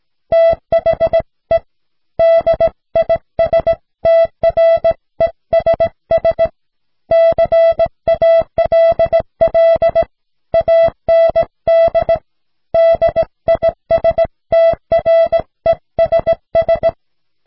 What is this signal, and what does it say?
Morse code, heard as a keyed tone, about 17 wpm, 660 Hz. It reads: THE DISTRESS CALL AND DISTRESS